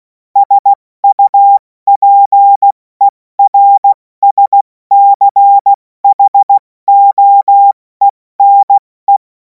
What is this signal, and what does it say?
Morse code, heard as a keyed tone, about 16 wpm, 800 Hz. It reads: SUPERSCHOENE
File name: Morsekurs / Witze / Witz 11. August 2026 70 F80